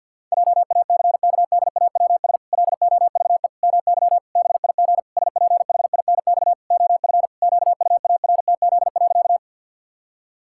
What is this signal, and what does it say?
Morse code, heard as a keyed tone, about 52 wpm, 700 Hz. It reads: JACKDAWS LOVE MY BIG SPHINX OF QUARTZ.
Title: Pangram Challenge v2 - 50wpm